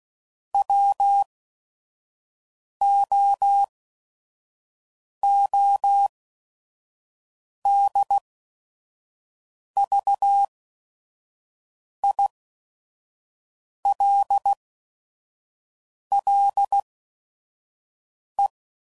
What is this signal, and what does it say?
Morse code, heard as a keyed tone, about 16 wpm, spaced as for 5 wpm, 800 Hz. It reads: WOODVILLE
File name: CW Code Groups @5WPM / fromSM's / Cities